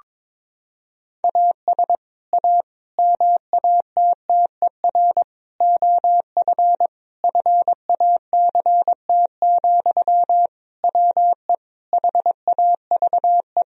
Morse code, heard as a keyed tone, about 21 wpm, 700 Hz. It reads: AS A MATTER OF FACT, WE HAVE
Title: Sentences from Top 1000 Words - Review Part 3 - 22wpm